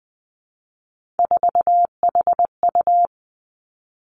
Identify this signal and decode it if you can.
Morse code, heard as a keyed tone, about 20 wpm, 700 Hz. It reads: 4HU
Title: Three Challenging Letter-Number Combinations 20wpm